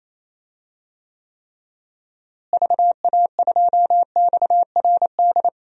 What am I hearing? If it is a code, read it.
Morse code, heard as a keyed tone, about 28 wpm, 700 Hz. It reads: VA2XRD
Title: DXCC-100 Call Signs - 6 Characters - Repeated 3x - 28wpm